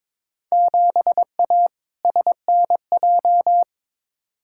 Morse code, heard as a keyed tone, about 22 wpm, 700 Hz. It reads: 7A SNJ
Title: Field Day Exchanges Encore - 22wpm